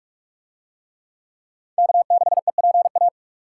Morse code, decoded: K6EPA

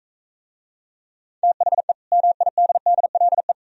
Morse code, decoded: THE MIDDLE

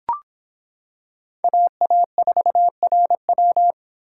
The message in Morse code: AA4RW

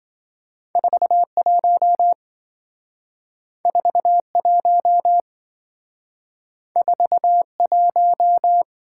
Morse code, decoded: 41 41 41

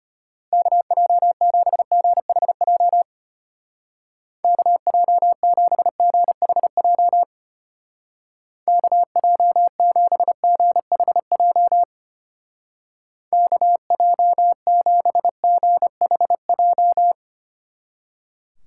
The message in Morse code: KJ7GHJ KJ7GHJ KJ7GHJ KJ7GHJ

